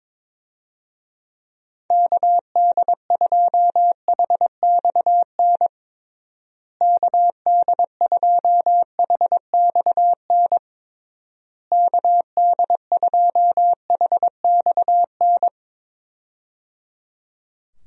Morse code, heard as a keyed tone, about 22 wpm, 700 Hz. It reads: KD2HXN KD2HXN KD2HXN
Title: USA Call Signs - 6 Characters - Repeated 3x - 22wpm